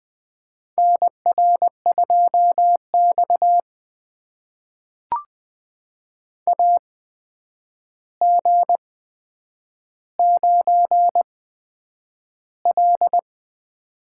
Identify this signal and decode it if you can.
Morse code, heard as a keyed tone, about 20 wpm, spaced as for 9 wpm, 700 Hz. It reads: NR2X AG9L
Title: USA Call Signs - 4 Characters - 8x Character Spacing - 20wpm